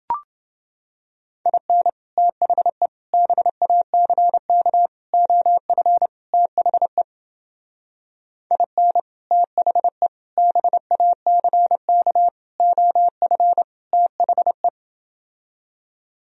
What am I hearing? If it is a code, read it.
Morse code, heard as a keyed tone, about 29 wpm, 700 Hz. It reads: IN THE BACK OF THE IN THE BACK OF THE